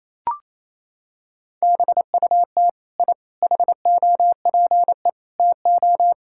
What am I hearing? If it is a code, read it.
Morse code, heard as a keyed tone, about 28 wpm, 700 Hz. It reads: BUT I HOPE TO
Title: Sentences from Top 1000 Words - Review Part 6 - 28wpm